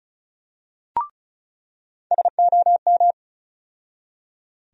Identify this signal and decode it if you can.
Morse code, heard as a keyed tone, about 34 wpm, 700 Hz. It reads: SOM
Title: USA Call Sign Postfixes - 3 Letters 35wpm